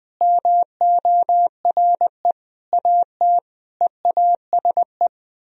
Morse code, heard as a keyed tone, about 20 wpm, 700 Hz. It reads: MORE AT EASE
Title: Sentences from Top 1000 Words - Review Part 1 - 20wpm